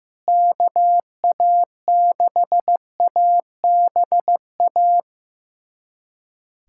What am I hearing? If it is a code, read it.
Morse code, heard as a keyed tone, about 15 wpm, 700 Hz. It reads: KA6ABA
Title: USA Callsign DE Callsign - 6 Characters - 15wpm